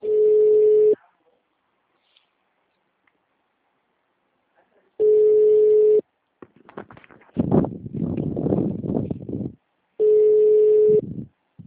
Ringback tone (Europe, 425 Hz).